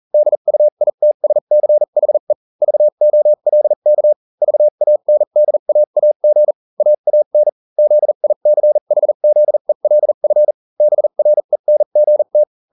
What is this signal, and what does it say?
Morse code, heard as a keyed tone, about 40 wpm, spaced as for 35 wpm, 600 Hz. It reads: DUITSCHE VOLK VANDAAG AAN ZICHZELF BRENGT